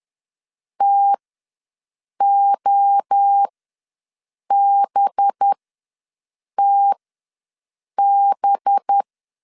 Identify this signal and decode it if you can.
Morse code, heard as a keyed tone, about 10 wpm, spaced as for 6 wpm, 800 Hz. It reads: TOBTB
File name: LEZIONI / 5 LBVC / 5 LB ws 10wpm 3616